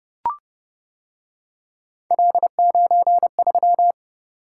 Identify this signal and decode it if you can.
Morse code, heard as a keyed tone, about 29 wpm, 700 Hz. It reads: L93